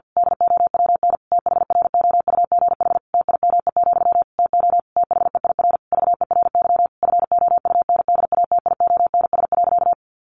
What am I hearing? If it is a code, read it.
Morse code, heard as a keyed tone, about 50 wpm, 700 Hz. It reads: DOWN THROUGH TIME, TO THEIR VERY FOUNDATIONS.
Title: A Sound Of Thunder 50wpm